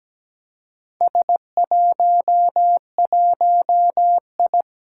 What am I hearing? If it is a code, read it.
Morse code, heard as a keyed tone, about 17 wpm, 700 Hz. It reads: S11I
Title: Four Challenging Letter-Number Combinations 17wpm